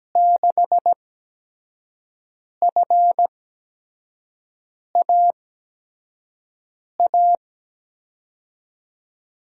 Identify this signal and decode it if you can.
Morse code, heard as a keyed tone, about 17 wpm, spaced as for 5 wpm, 700 Hz. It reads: 6FAA
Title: USA Call Signs - 6 Characters - 8x Character Spacing - 17wpm